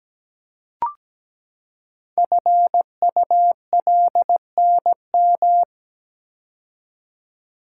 Morse code, heard as a keyed tone, about 17 wpm, 700 Hz. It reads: FULNM
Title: Code Groups - 5 Letters - 17wpm